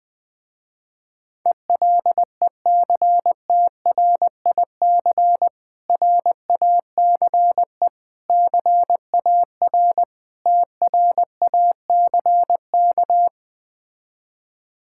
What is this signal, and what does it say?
Morse code, heard as a keyed tone, about 20 wpm, 700 Hz. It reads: ELECTRIC RACE CAR TRACK
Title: Difficult Words with Hints - 20wpm